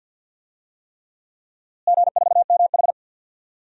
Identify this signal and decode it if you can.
Morse code, heard as a keyed tone, about 52 wpm, 700 Hz. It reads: G4MH
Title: DXCC-100 Call Signs - 4 Characters - Repeated 3x - 50wpm